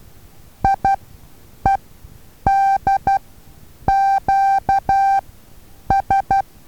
Morse code, heard as a keyed tone, about 12 wpm, spaced as for 8 wpm, 780 Hz. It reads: IEDQS
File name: Taviro HA6VA A